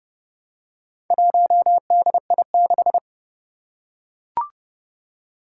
Morse code, heard as a keyed tone, about 29 wpm, 700 Hz. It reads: 1DI6
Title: Four Challenging Letter-Number Combinations 30wpm